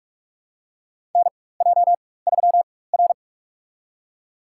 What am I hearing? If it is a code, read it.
Morse code, heard as a keyed tone, about 47 wpm, spaced as for 21 wpm, 700 Hz. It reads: NJ3R
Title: USA Call Signs - 4 Characters - 4x Character Spacing - 45wpm